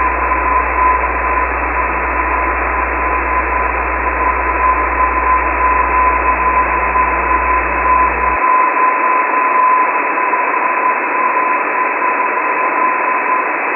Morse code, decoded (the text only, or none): none